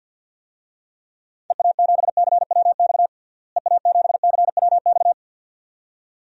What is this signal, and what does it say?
Morse code, heard as a keyed tone, about 53 wpm, 700 Hz. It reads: EA7CWX EA7CWX
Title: DXCC-100 Call Signs - 6 Characters - Repeated 2x - 50wpm